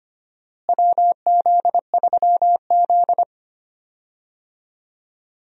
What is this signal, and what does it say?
Morse code, heard as a keyed tone, about 25 wpm, 700 Hz. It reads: WZ3Z
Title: USA Call Signs - 4 Characters - Repeated 2x - 25wpm